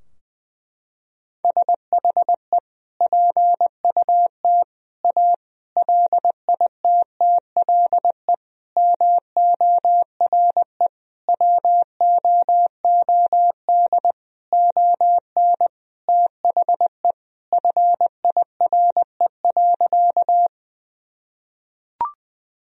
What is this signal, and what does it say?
Morse code, heard as a keyed tone, about 20 wpm, 700 Hz. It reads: SHE PUT A LITTLE MORE WOOD ON THE FIRE.
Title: Sentences from Top 1000 Words - Review Part 2 - 20wpm